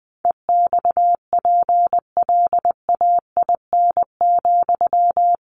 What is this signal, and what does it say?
Morse code, heard as a keyed tone, about 20 wpm, 700 Hz. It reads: EXPLAIN,
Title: Sentences from Top 1100 Words 20wpm